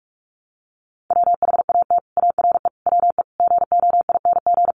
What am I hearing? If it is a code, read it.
Morse code, heard as a keyed tone, about 44 wpm, 700 Hz. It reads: WHAT ARE WE GOING
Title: Sets of 5 Words 45wpm